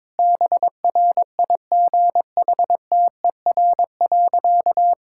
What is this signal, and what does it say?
Morse code, heard as a keyed tone, about 22 wpm, 700 Hz. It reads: BRIGHTER.
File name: Difficult Sentences with Hints - 22wpm